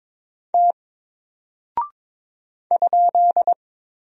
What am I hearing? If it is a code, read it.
Morse code, heard as a keyed tone, about 22 wpm, 700 Hz. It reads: T ?
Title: Single Character (Letter:Number:Punctuation:Prosign) Alt 22wpm